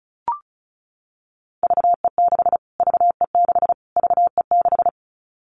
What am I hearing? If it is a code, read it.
Morse code, heard as a keyed tone, about 34 wpm, 700 Hz. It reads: VE6 VE6 VE6
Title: Callsign Prefixes for Top 25 DXCC - Repeated 3x - 35wpm